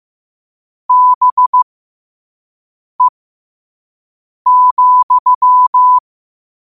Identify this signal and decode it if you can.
Morse code, heard as a keyed tone, about 15 wpm, spaced as for 5 wpm, 1000 Hz. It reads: BE,
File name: 5wpm